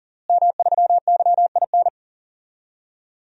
Morse code, heard as a keyed tone, about 40 wpm, 700 Hz. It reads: M3YIN
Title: DXCC-100 Call Signs - 5 Characters - Repeated 2x - 40wpm